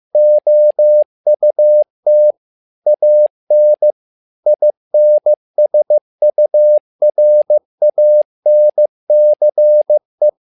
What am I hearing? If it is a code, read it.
Morse code, heard as a keyed tone, about 15 wpm, 600 Hz. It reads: OUT AN INSURANCE
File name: learning center / tower trouble / tower trouble 15wpm